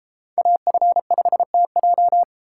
Morse code, heard as a keyed tone, about 33 wpm, 700 Hz. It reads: AF5TJ